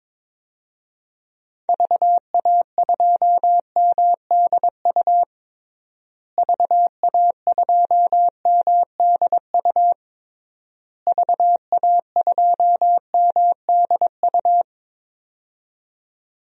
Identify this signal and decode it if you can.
Morse code, heard as a keyed tone, about 22 wpm, 700 Hz. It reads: VA2MDU VA2MDU VA2MDU
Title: DXCC-100 Call Signs - 6 Characters - Repeated 3x - 22wpm